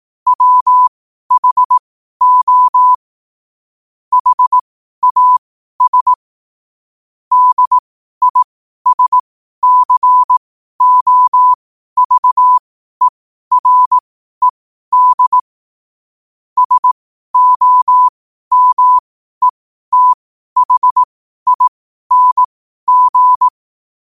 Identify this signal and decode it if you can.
Morse code, heard as a keyed tone, about 18 wpm, spaced as for 12 wpm, 1000 Hz. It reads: WHO HAS DISCOVERED SOMETHING
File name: qotdfiles / 12wpm